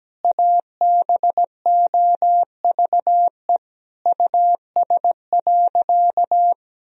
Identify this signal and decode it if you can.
Morse code, heard as a keyed tone, about 17 wpm, 700 Hz. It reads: ABOVE US.
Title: Sentences from Top 1000 Words - Review Part 2 - 17wpm